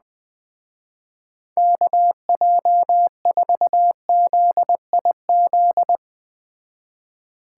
Morse code, heard as a keyed tone, about 20 wpm, 700 Hz. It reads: KJ4ZIZ
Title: Speed Racing - USA 2x3 Call Signs 17wpm